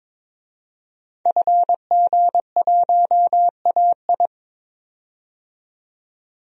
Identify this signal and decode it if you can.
Morse code, heard as a keyed tone, about 22 wpm, 700 Hz. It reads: FG1AI
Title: DXCC-100 Call Signs - 5 Characters - 22wpm